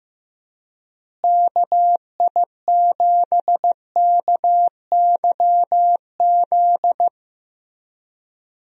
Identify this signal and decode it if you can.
Morse code, heard as a keyed tone, about 15 wpm, 700 Hz. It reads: KI7KYZ
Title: USA Call Signs - 6 Characters - Repeated 2x - 15wpm